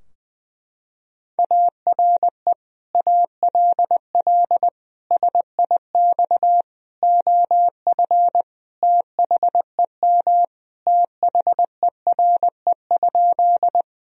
Morse code, heard as a keyed tone, about 21 wpm, 700 Hz. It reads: ARE ALL SIX OF THEM THERE?
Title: Sentences from Top 1000 Words - Review Part 2 - 20wpm